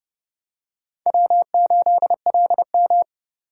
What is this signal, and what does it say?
Morse code, heard as a keyed tone, about 29 wpm, 700 Hz. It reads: W8LM